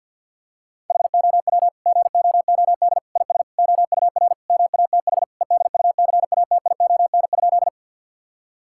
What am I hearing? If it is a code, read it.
Morse code, heard as a keyed tone, about 52 wpm, 700 Hz. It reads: HOW GOOD IS OUR MATH EDUCATION?
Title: Sentences from Top 1000 Words - Review Part 6 - 50wpm